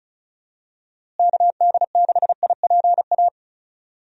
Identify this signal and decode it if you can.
Morse code, heard as a keyed tone, about 34 wpm, 700 Hz. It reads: KD6IPA
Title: USA Call Signs - 6 Characters - Repeated 2x - 35wpm